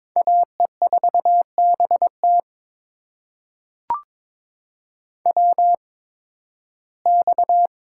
Morse code, heard as a keyed tone, about 22 wpm, spaced as for 14 wpm, 700 Hz. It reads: AE4BT WX
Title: USA Call Signs - 5 Characters - 8x Character Spacing - 22wpm